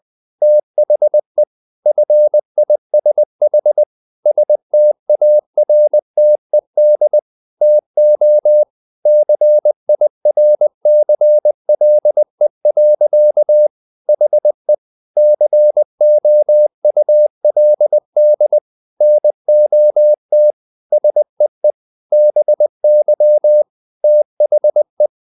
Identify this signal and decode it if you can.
Morse code, heard as a keyed tone, about 20 wpm, 600 Hz. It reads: THE FISH STARTED TO CIRCLE. HE COULD NOT SEE BY THE
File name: the old man and the sea part 2 20wpm